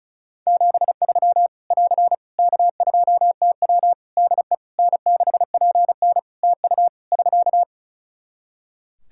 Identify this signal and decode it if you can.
Morse code, heard as a keyed tone, about 36 wpm, 700 Hz. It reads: 73 <AR> K2TW DE N6PN TU <SK>
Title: Practice QSO - Exchange 4 - Ending - 35wpm